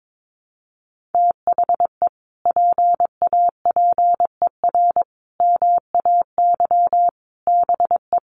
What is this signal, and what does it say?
Morse code, heard as a keyed tone, about 22 wpm, 700 Hz. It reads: THE PAPER MAY BE
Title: Sentences from Top 1400 Words 22wpm